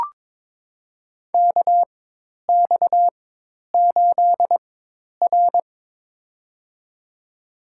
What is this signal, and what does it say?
Morse code, heard as a keyed tone, about 22 wpm, spaced as for 10 wpm, 700 Hz. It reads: KX8R
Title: USA Call Signs - 4 Characters - 4x Character Spacing - 22wpm